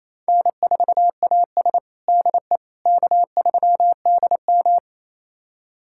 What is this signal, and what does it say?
Morse code, heard as a keyed tone, about 28 wpm, 700 Hz. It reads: N4AS DE K3DM
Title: USA Callsign DE Callsign - 4 Characters - 28wpm